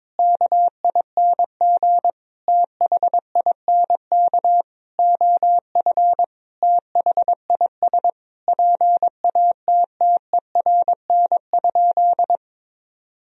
Morse code, heard as a keyed tone, about 22 wpm, 700 Hz. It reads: KING THINK OF THIS PATTERN?
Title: Sentences from Top 1000 Words - Review Part 2 - 22wpm